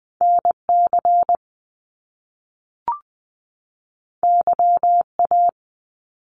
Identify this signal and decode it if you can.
Morse code, heard as a keyed tone, about 20 wpm, 700 Hz. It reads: NC YA
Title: Top Two Letter Combinations in Words 20wpm